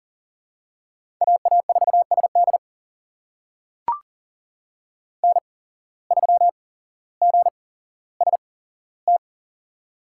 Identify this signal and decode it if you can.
Morse code, heard as a keyed tone, about 40 wpm, spaced as for 17 wpm, 700 Hz. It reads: AA4SD N3GST
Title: USA Call Signs - 5 Characters - 8x Character Spacing - 40wpm